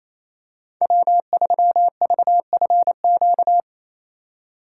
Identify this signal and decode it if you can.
Morse code, heard as a keyed tone, about 28 wpm, 700 Hz. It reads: W3VFQ
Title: USA Call Signs - 5 Characters - 2x Character Spacing - 28wpm